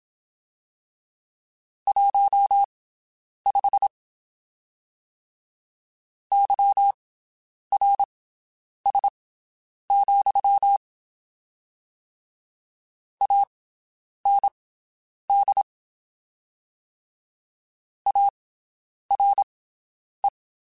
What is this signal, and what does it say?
Morse code, heard as a keyed tone, about 27 wpm, spaced as for 9 wpm, 800 Hz. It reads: 15 YRS, AND ARE
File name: (spaced) / QSOs Spaced-mp3s / WD0BGP 1 @25 sp